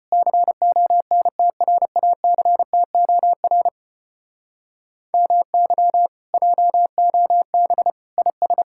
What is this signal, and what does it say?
Morse code, heard as a keyed tone, about 33 wpm, 700 Hz. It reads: CONTRACTOR MY JOB IS